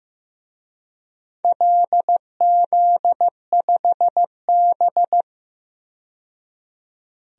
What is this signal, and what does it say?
Morse code, heard as a keyed tone, about 15 wpm, 700 Hz. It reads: LZ5B
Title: DXCC-100 Call Signs - 4 Characters - 15wpm